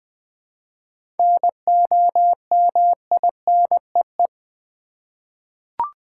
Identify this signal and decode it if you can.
Morse code, heard as a keyed tone, about 20 wpm, 700 Hz. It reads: NOMINEE
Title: 7 Letter Words - First 3 Letters Given - 20wpm